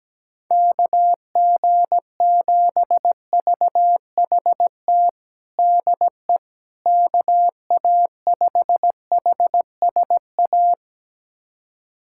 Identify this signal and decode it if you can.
Morse code, heard as a keyed tone, about 17 wpm, 700 Hz. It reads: KG7VHT DE KA5HSA
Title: USA Callsign DE Callsign - 6 Characters - 17wpm